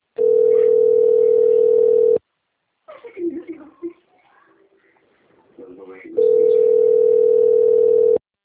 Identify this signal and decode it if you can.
Ringback tone (North America), 440 + 480 Hz, with speech.
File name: rg-1600-unknown- - - .29598